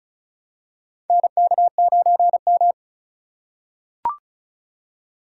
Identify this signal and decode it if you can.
Morse code, heard as a keyed tone, about 36 wpm, 700 Hz. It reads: NK9M
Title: USA Call Signs - 8x Character Spacing - 35wpm